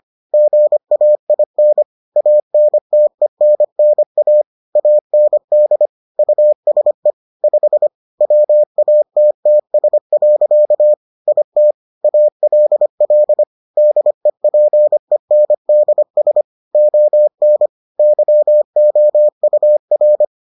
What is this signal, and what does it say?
Morse code, heard as a keyed tone, about 25 wpm, 600 Hz. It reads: GAIN ANTENNA AND USE 5 WATTS. IT ALL DEPENDS ON YOUR